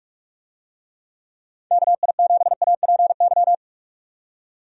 Morse code, heard as a keyed tone, about 46 wpm, 700 Hz. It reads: KI7APY
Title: USA Call Signs - 6 Characters - 45wpm